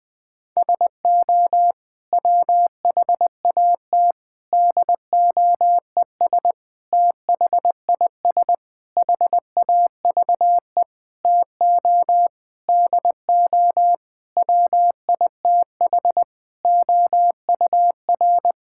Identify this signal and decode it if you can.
Morse code, heard as a keyed tone, about 20 wpm, 700 Hz. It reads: SO WHAT DOES THIS HAVE TO DO WITH OUR
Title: Sentences with 8-Word Phrases - 20wpm